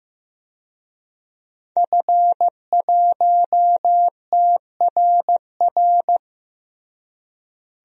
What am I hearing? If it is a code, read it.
Morse code, heard as a keyed tone, about 16 wpm, 700 Hz. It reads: F1TRR
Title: DXCC-100 Call Signs - 5 Characters - Repeated 3x - 15wpm